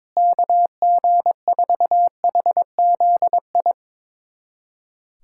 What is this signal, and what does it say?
Morse code, heard as a keyed tone, about 22 wpm, 700 Hz. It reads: KG4HZI
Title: USA Callsign DE Callsign 22wpm